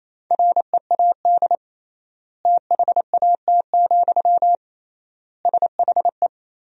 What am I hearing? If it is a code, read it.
Morse code, heard as a keyed tone, about 28 wpm, 700 Hz. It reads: READ THAT, SHE
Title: 3x Word-Spacing - Sentences from Top 200 Words - 28wpm